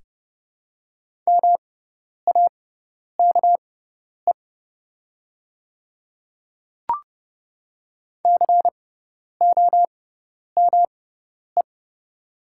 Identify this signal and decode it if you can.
Morse code, heard as a keyed tone, about 30 wpm, spaced as for 10 wpm, 700 Hz. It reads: MAKE COME
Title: 6x Character Spacing - Top 100 Words - 30wpm